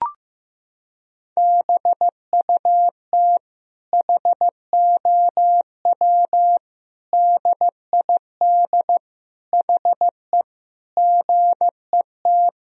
Morse code, heard as a keyed tone, about 15 wpm, 700 Hz. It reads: BUT HOW DID HE GET
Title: Sentences from Top 1000 Words - Review Part 4 - 15wpm